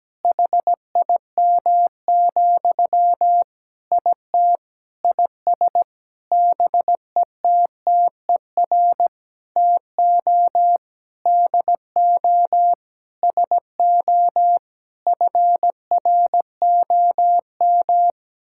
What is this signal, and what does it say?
Morse code, heard as a keyed tone, about 17 wpm, 700 Hz. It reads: HIM, IT IS BETTER TO DO SO FROM
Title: Sentences from Top 1600 Words 17wpm